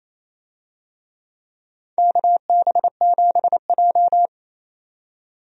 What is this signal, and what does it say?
Morse code, heard as a keyed tone, about 27 wpm, 700 Hz. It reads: KB7J
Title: USA Call Signs - 4 Characters - 28wpm